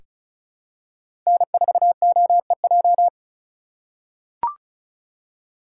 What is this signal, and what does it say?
Morse code, heard as a keyed tone, about 34 wpm, 700 Hz. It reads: N4OEJ
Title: USA Call Signs - 5 Characters - 4x Character Spacing - 35wpm